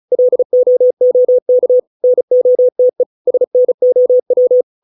Morse code, heard as a keyed tone, about 34 wpm, 500 Hz. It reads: LOOK NOTE SNOW